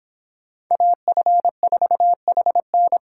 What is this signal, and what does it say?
Morse code, heard as a keyed tone, about 31 wpm, 700 Hz. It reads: AF4HN